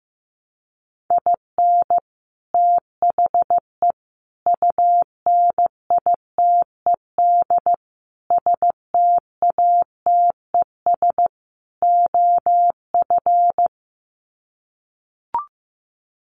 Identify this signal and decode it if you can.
Morse code, heard as a keyed tone, about 15 wpm, 700 Hz. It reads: IN THE UNITED STATES OF
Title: Sets of 5 Words Encore2 15wpm